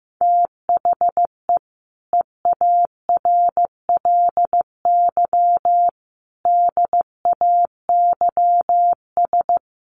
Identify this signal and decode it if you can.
Morse code, heard as a keyed tone, about 15 wpm, 700 Hz. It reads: THE EARLY DAYS
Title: Sets of 5 Words Encore 15wpm